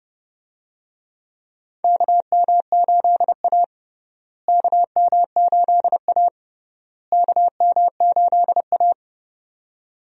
Morse code, heard as a keyed tone, about 29 wpm, 700 Hz. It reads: KM8A KM8A KM8A
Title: USA Call Signs - 4 Characters - Repeated 3x - 30wpm